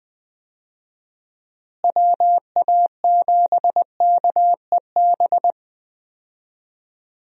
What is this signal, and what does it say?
Morse code, heard as a keyed tone, about 20 wpm, 700 Hz. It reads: WA7KEB
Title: USA Call Signs - 6 Characters - Repeated 2x - 20wpm